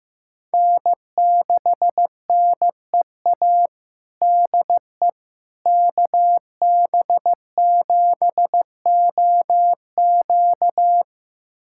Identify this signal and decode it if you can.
Morse code, heard as a keyed tone, about 15 wpm, 700 Hz. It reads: N6NEA DE KB7OQ
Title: USA Callsign DE Callsign - 5 Characters - 15wpm